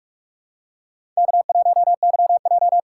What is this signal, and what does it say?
Morse code, heard as a keyed tone, about 43 wpm, 700 Hz. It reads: K1YJ
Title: USA Call Signs - 4 Characters - 3x Character Spacing - 45wpm